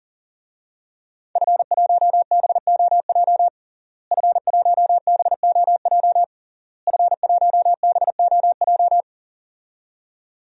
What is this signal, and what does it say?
Morse code, heard as a keyed tone, about 41 wpm, 700 Hz. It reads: F1BOJ F1BOJ F1BOJ
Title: DXCC-100 Call Signs - 5 Characters - Repeated 3x - 40wpm